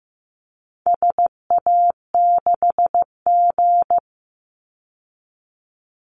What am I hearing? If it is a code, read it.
Morse code, heard as a keyed tone, about 15 wpm, 700 Hz. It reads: SA6G
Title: Call Signs Repeated 3x - 15wpm